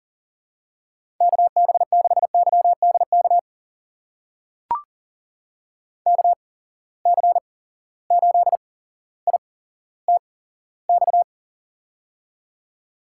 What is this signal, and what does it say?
Morse code, heard as a keyed tone, about 40 wpm, spaced as for 17 wpm, 700 Hz. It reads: KB6YDK KC8ITX
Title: USA Call Signs - 6 Characters - 8x Character Spacing - 40wpm